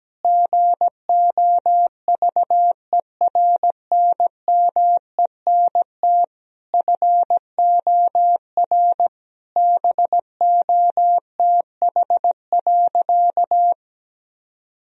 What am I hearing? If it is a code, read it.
Morse code, heard as a keyed tone, about 17 wpm, 700 Hz. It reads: GOVERNMENT FOR BOTH.
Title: Sentences from Top 1000 Words - Review Part 3 - 17wpm